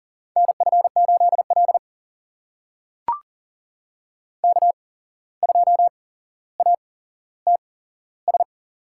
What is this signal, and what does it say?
Morse code, heard as a keyed tone, about 40 wpm, spaced as for 16 wpm, 700 Hz. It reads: NF8L K2ATS
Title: USA Call Signs - 8x Character Spacing - 40wpm